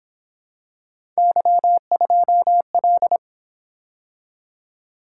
Morse code, heard as a keyed tone, about 26 wpm, 700 Hz. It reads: Y2L
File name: Speed-Racing - Three Letter-Number Combinations - 22wpm